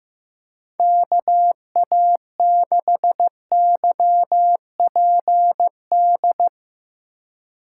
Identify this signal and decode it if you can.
Morse code, heard as a keyed tone, about 15 wpm, 700 Hz. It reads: KA6YPD
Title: USA Call Signs - 6 Characters - 4x Character Spacing - 15wpm